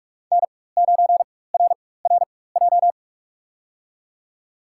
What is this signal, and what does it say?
Morse code, heard as a keyed tone, about 46 wpm, spaced as for 21 wpm, 700 Hz. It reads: N9RRJ
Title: USA Call Signs - 5 Characters - 4x Character Spacing - 45wpm